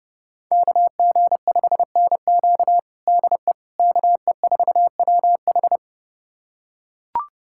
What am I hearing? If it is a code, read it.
Morse code, heard as a keyed tone, about 28 wpm, 700 Hz. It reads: KG5NQ DE KE4WH
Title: USA Callsign DE Callsign 30wpm